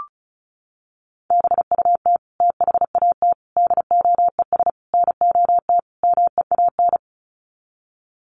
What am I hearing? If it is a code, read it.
Morse code, heard as a keyed tone, about 34 wpm, 700 Hz. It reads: BUT THAT DOES NOT MEAN
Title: Sets of 5 Words 35wpm